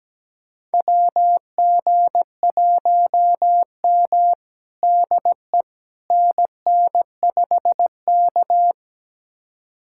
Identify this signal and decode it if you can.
Morse code, heard as a keyed tone, about 17 wpm, 700 Hz. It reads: WG1M DE NN5K
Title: USA Callsign DE Callsign - 4 Characters - 17wpm